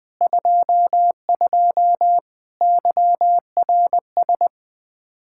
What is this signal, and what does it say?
Morse code, heard as a keyed tone, about 21 wpm, 700 Hz. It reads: 22 YRS
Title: Practice QSO - Exchange 3 - More Info - 20wpm